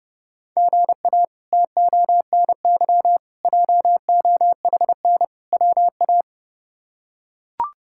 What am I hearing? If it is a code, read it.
Morse code, heard as a keyed tone, about 32 wpm, 700 Hz. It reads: GA TONY JOHN WA
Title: K1USN SST QSO Practice - 30wpm